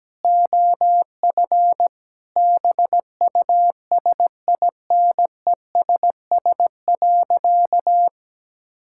Morse code, heard as a keyed tone, about 17 wpm, 700 Hz. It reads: OF BUSINESS.
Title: Sentences from Top 1000 Words - Review Part 5 - 17wpm